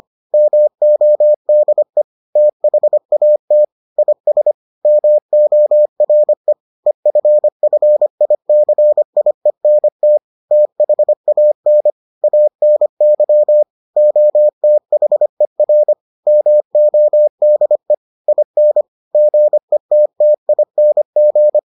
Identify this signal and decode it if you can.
Morse code, heard as a keyed tone, about 25 wpm, 600 Hz. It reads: MODE THAT IS MORE EFFICIENT THAN ANY OTHER MODE IN GETTING